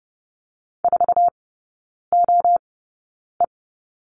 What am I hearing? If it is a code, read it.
Morse code, heard as a keyed tone, about 31 wpm, spaced as for 9 wpm, 690 Hz. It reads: 4OE
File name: Lesson 16e - Letter c - Three Characters 30wpm